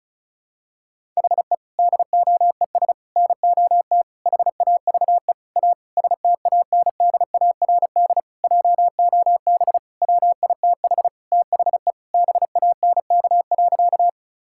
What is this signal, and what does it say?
Morse code, heard as a keyed tone, about 34 wpm, 700 Hz. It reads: HE DOES NOT HAVE A STANDARD JOB WITH THE BANK.
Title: Sentences from Top 1000 Words - Review Part 6 - 35wpm